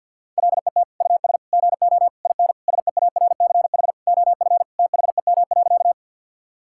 Morse code, heard as a keyed tone, about 50 wpm, 700 Hz. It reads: LET US GO IN SEARCH OF THEM.
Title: Sentences from Top 1000 Words - Review Part 5 - 50wpm